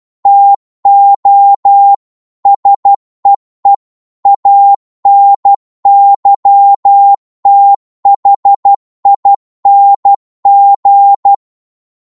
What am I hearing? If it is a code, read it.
Morse code, heard as a keyed tone, about 12 wpm, 800 Hz. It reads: TO SEE ANYTHING
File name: cwmp3 / psg-1-12wpm